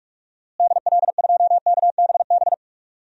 Morse code, heard as a keyed tone, about 44 wpm, 700 Hz. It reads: DL2KBB
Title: DXCC-100 Call Signs - 6 Characters - Repeated 3x - 45wpm